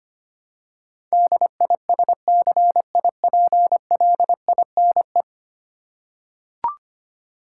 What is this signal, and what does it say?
Morse code, heard as a keyed tone, about 25 wpm, 700 Hz. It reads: DISCIPLINE
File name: Ten Letter Words - 25wpm